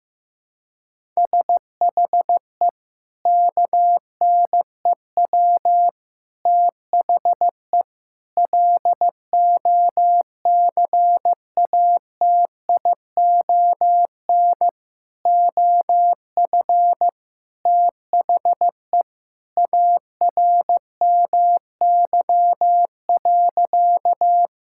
Morse code, heard as a keyed tone, about 15 wpm, 700 Hz. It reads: SHE KNEW THE LOCATION OF THE ARMY.
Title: Sentences from Top 1000 Words - Review Part 6 - 15wpm